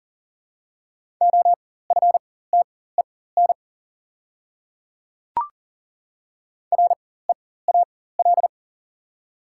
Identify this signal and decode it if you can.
Morse code, heard as a keyed tone, about 40 wpm, spaced as for 19 wpm, 700 Hz. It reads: OFTEN REAL
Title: 4x Character Spacing - Top 300 Words - 40wpm